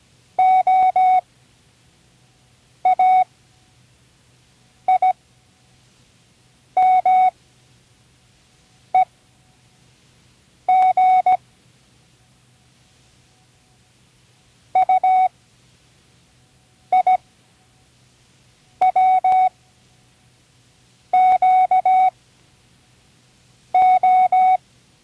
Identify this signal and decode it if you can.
Morse code, heard as a keyed tone, about 17 wpm, spaced as for 5 wpm, 740 Hz. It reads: OAIMEG UIWQO
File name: Learning Morse Code (from Tune in the World)